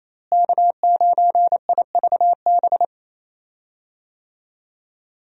Morse code, heard as a keyed tone, about 28 wpm, 700 Hz. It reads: K9IVB
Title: USA Call Signs - 5 Characters - Repeated 3x - 28wpm